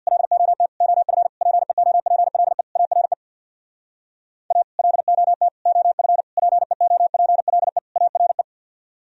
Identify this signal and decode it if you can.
Morse code, heard as a keyed tone, about 65 wpm, 700 Hz. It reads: LOT OF PEOPLE ARE A LOT OF PEOPLE ARE